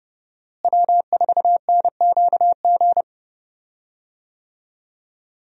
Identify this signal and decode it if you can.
Morse code, heard as a keyed tone, about 30 wpm, 700 Hz. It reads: W4NQG
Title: USA Call Signs - 5 Characters - Repeated 2x - 30wpm